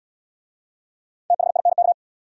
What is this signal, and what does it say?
Morse code, heard as a keyed tone, about 65 wpm, 700 Hz. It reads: THINK